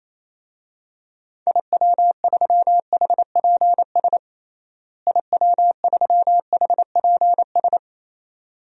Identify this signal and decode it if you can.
Morse code, heard as a keyed tone, about 28 wpm, 700 Hz. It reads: IW3HPS IW3HPS
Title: DXCC-100 Call Signs - 6 Characters - Repeated 2x - 28wpm